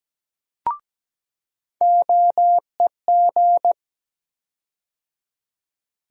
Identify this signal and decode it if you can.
Morse code, heard as a keyed tone, about 17 wpm, 700 Hz. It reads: OEG
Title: USA Call Sign Postfixes - 3 Letters 17wpm